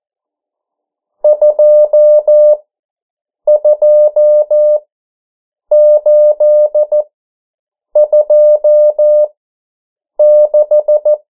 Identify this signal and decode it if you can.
Morse code, heard as a keyed tone, about 14 wpm, spaced as for 7 wpm, 600 Hz. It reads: 22826
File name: morse mp3 / 09 urok 8